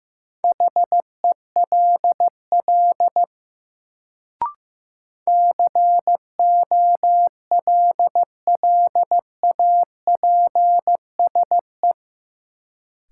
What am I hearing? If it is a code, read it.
Morse code, heard as a keyed tone, about 15 wpm, 700 Hz. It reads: HELL COLLAPSE
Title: Sets of 5 Words Encore - New Word Drill - 15wpm